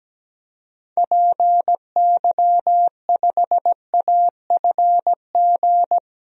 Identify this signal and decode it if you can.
Morse code, heard as a keyed tone, about 17 wpm, 700 Hz. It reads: PY5AFG
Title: DXCC-100 Call Signs - 6 Characters - 17wpm